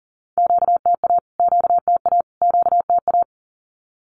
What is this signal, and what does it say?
Morse code, heard as a keyed tone, about 40 wpm, 700 Hz. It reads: QTA QTA QTA
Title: Q-Codes Repeated 3x 40wpm